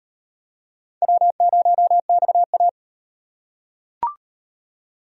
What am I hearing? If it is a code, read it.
Morse code, heard as a keyed tone, about 37 wpm, 700 Hz. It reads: W0XA